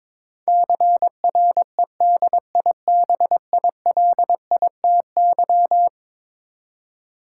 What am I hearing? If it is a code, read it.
Morse code, heard as a keyed tone, about 22 wpm, 700 Hz. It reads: CREDIBILITY
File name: practice / 22wpm / Eleven Letter Words - 22wpm